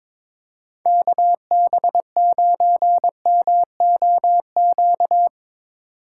Morse code, heard as a keyed tone, about 22 wpm, 700 Hz. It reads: KB9MOQ
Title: USA Call Signs - 6 Characters - 6x Character Spacing - 22wpm